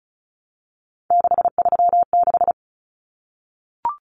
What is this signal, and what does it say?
Morse code, heard as a keyed tone, about 34 wpm, 700 Hz. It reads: 636
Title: Numbers - 3-Digit 35wpm